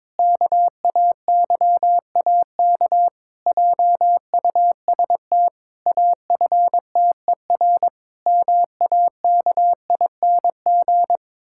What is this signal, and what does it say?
Morse code, heard as a keyed tone, about 22 wpm, 700 Hz. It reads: KAYAK JUST AFTER MAKING
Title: Pangram Challenge v2 - 22wpm